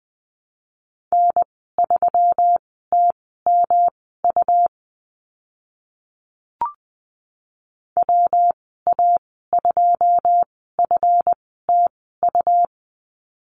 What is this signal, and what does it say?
Morse code, heard as a keyed tone, about 20 wpm, spaced as for 14 wpm, 700 Hz. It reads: N3TMU WA2FTU
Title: 140 Lesson 23g - Letter f - Call Signs 20wpm